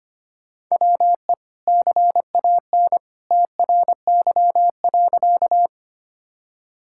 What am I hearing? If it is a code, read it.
Morse code, heard as a keyed tone, about 25 wpm, 700 Hz. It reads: WE CAN TRY.
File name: practice / 25wpm / Sentences with 2-Word Phrases - 25wpm